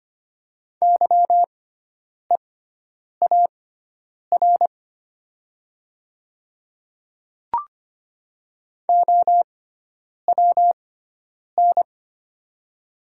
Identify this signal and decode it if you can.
Morse code, heard as a keyed tone, about 25 wpm, spaced as for 9 wpm, 690 Hz. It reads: YEAR OWN
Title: 6x Character Spacing - Top 300 Words - 25wpm